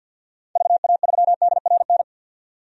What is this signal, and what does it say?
Morse code, heard as a keyed tone, about 46 wpm, 700 Hz. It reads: VA3DRN
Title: DXCC-100 Call Signs - 6 Characters - 50wpm